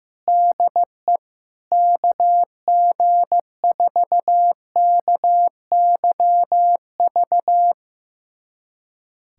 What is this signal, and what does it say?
Morse code, heard as a keyed tone, about 15 wpm, 700 Hz. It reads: DE KG4KYV
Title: USA Callsign DE Callsign - 6 Characters - 15wpm